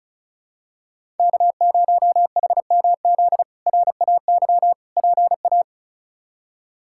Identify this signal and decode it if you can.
Morse code, heard as a keyed tone, about 34 wpm, 700 Hz. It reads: K0HMZ RAY PA
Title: K1USN SST QSO Practice - 35wpm